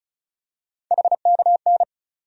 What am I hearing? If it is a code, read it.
Morse code, heard as a keyed tone, about 34 wpm, 700 Hz. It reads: HKN
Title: USA Call Sign Postfixes - 3 Letters 35wpm